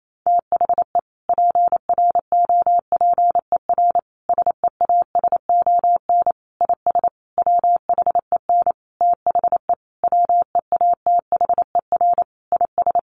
Morse code, heard as a keyed tone, about 28 wpm, 700 Hz. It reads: THE PROPER SEASON IS WHEN THE WEATHER IS
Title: Sentences from Top 1400 Words 28wpm